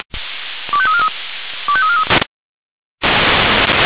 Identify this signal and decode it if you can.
Five-tone selective call (ZVEI), heard as quick runs of five tones, tones of about 70 ms. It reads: 25343 25343